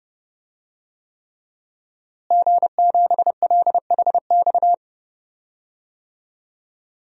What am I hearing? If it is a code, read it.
Morse code, heard as a keyed tone, about 30 wpm, 700 Hz. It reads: G7LHX
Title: DXCC-100 Call Signs - 5 Characters - Repeated 3x - 30wpm